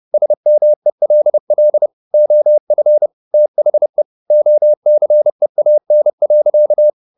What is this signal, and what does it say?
Morse code, heard as a keyed tone, about 30 wpm, 600 Hz. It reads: SMELL OF THE OCEAN.
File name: the old man and the sea / the old man and the sea 1 30wpm